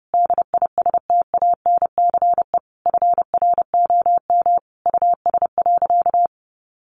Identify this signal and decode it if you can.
Morse code, heard as a keyed tone, about 30 wpm, 700 Hz. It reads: DISTANCE FROM US.
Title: Sentences from Top 1200 Words 30wpm